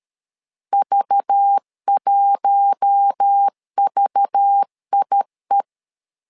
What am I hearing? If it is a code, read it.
Morse code, heard as a keyed tone, about 13 wpm, 800 Hz. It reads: V1VIE
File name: LEZIONI / 5 LBVC / 5 LBV 12wpm 1855